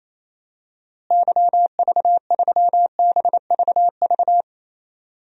Morse code, heard as a keyed tone, about 27 wpm, 700 Hz. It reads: YV3BVV
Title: DXCC-100 Call Signs - 6 Characters - Repeated 3x - 28wpm